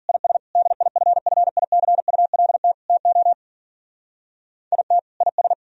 Morse code, heard as a keyed tone, about 50 wpm, 700 Hz. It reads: IS DIFFICULT TO IT IS